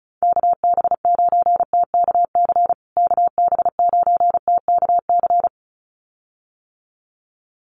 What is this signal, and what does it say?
Morse code, heard as a keyed tone, about 34 wpm, 700 Hz. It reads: KB9TKC KB9TKC
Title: USA Call Signs Repeated 2x 35wpm